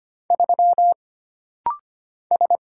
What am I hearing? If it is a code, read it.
Morse code, heard as a keyed tone, about 25 wpm, 700 Hz. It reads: 3 S
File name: Single Character (Letter:Number:Punctuation:Prosign) - Rapid-Fire - 25wpm